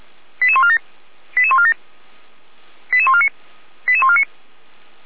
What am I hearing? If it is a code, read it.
Five-tone selective call (ZVEI), heard as quick runs of five tones, tones of about 70 ms. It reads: 88147 88147 88149 88149